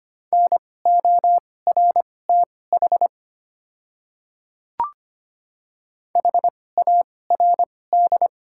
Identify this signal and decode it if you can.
Morse code, heard as a keyed tone, about 25 wpm, spaced as for 18 wpm, 690 Hz. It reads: NORTH HARD
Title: 2x Character Spacing - Top 300 Words - 25wpm